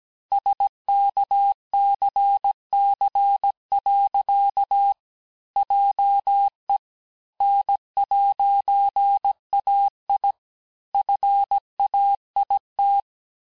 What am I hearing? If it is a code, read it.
Morse code, heard as a keyed tone, about 17 wpm, 800 Hz. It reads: SKCC. JE N'AI FAIT
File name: corrige-cours-1581-fin-du-QSO-vitesse-16-mpm